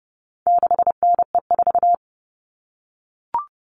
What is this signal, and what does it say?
Morse code, heard as a keyed tone, about 30 wpm, 700 Hz. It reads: 6NE4
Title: All Cut Numbers - 4-Digit 30wpm